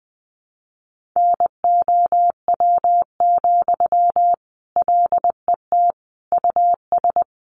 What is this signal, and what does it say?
Morse code, heard as a keyed tone, about 20 wpm, 700 Hz. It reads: NOW, LET US
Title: Sentences from Top 1500 Words 20wpm